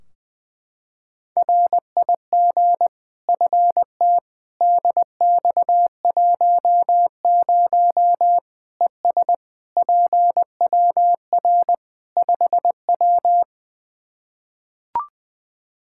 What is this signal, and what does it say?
Morse code, heard as a keyed tone, about 20 wpm, 700 Hz. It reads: RIG FT DX10 ES PWR 5W
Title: Practice QSO - Exchange 2 - Info - 20wpm